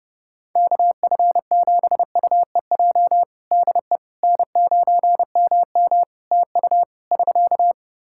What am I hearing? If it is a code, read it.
Morse code, heard as a keyed tone, about 32 wpm, 700 Hz. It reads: KF7UEJ DE N9MM TU <SK>
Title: Practice QSO - Exchange 4 - Ending - 30wpm